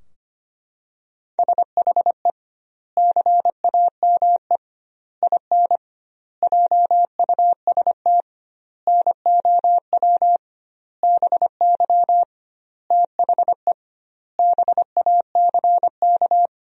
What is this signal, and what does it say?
Morse code, heard as a keyed tone, about 25 wpm, 700 Hz. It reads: SHE CAME IN JUST NOW BY THE BACK
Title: 2x Word-Spacing - Sentences from Top 300 Words - 25wpm